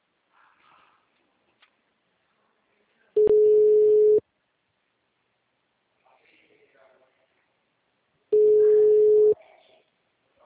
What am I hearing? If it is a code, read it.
Ringback tone (Europe, 425 Hz), with speech.